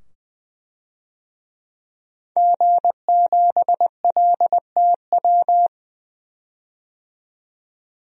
Morse code, heard as a keyed tone, about 20 wpm, 700 Hz. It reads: G7LTW
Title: DXCC-100 Call Signs - 5 Characters - 20wpm